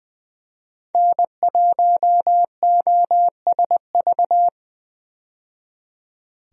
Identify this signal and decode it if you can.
Morse code, heard as a keyed tone, about 20 wpm, 700 Hz. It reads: N1OSV
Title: USA Call Signs - 5 Characters - 20wpm